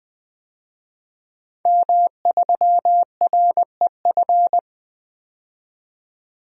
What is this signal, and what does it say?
Morse code, heard as a keyed tone, about 20 wpm, 700 Hz. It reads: M3REF
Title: DXCC-100 Call Signs - 5 Characters - 20wpm